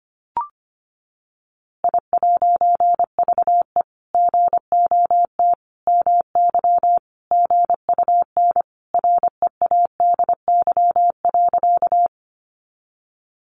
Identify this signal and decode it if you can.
Morse code, heard as a keyed tone, about 25 wpm, 700 Hz. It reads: I'VE GOT MY GUN READY.
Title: A Sound Of Thunder 25wpm